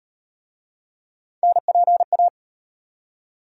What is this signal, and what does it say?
Morse code, heard as a keyed tone, about 35 wpm, 700 Hz. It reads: NPA